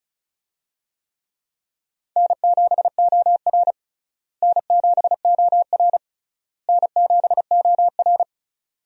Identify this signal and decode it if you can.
Morse code, heard as a keyed tone, about 36 wpm, 700 Hz. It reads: N7OR N7OR N7OR
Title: USA Call Signs - 4 Characters - Repeated 3x - 35wpm